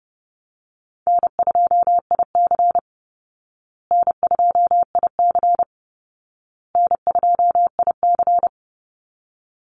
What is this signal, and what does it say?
Morse code, heard as a keyed tone, about 30 wpm, 700 Hz. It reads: N2IC N2IC N2IC
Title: Easy Call Signs Repeated 3x - 30wpm